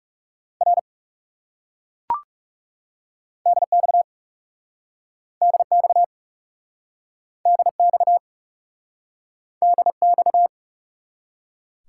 Morse code, heard as a keyed tone, about 40 wpm, 700 Hz. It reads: R DX DX DX DX